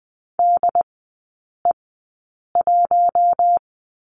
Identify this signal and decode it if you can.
Morse code, heard as a keyed tone, about 20 wpm, spaced as for 8 wpm, 700 Hz. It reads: DE1
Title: 067 Lesson 12e - Letter d - Three Characters 20wpm